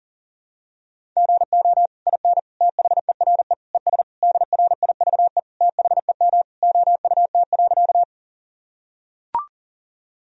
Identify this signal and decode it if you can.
Morse code, heard as a keyed tone, about 37 wpm, 700 Hz. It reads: GO IN THERE ES DRIVE THEM OUT.
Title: Sentences from Top 500 Words and Abbreviations 40wpm